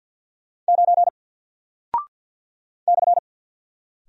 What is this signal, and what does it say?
Morse code, heard as a keyed tone, about 48 wpm, 700 Hz. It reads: 9 /
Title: Single Character (Letter:Number:Punctuation:Prosign) Alt 50wpm